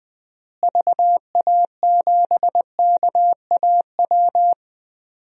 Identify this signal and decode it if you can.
Morse code, heard as a keyed tone, about 21 wpm, 700 Hz. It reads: VA7KAW
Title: DXCC-100 Call Signs - 6 Characters - Repeated 2x - 20wpm